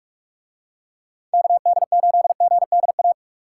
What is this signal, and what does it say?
Morse code, heard as a keyed tone, about 48 wpm, 700 Hz. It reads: KD8GDA